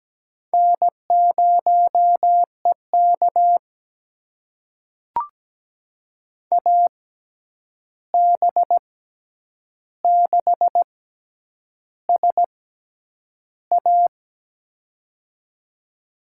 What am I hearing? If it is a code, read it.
Morse code, heard as a keyed tone, about 17 wpm, spaced as for 8 wpm, 700 Hz. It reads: N0EK AB6SA
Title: USA Call Signs - 6x Character Spacing - 17wpm